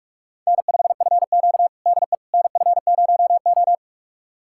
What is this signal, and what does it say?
Morse code, heard as a keyed tone, about 46 wpm, 700 Hz. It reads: N5FQ DE NF0O
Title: USA Callsign DE Callsign - 4 Characters - 45wpm